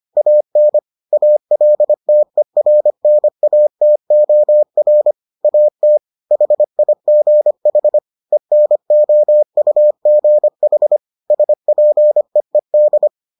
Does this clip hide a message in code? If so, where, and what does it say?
Yes: Morse code from 0.2 to 13.1 s, heard as a keyed tone, about 25 wpm, 600 Hz. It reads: AN ALTERNATOR AT HIGH ENOUGH SPEED